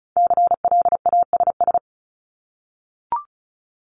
Morse code, heard as a keyed tone, about 34 wpm, 700 Hz. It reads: CLASS
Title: Top 500 Words Repeated 35wpm